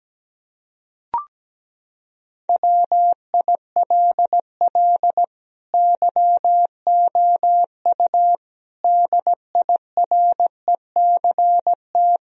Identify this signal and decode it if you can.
Morse code, heard as a keyed tone, about 17 wpm, 700 Hz. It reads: WILL YOU DIRECT
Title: Sentences from Top 1000 Words - Review Part 2 - 17wpm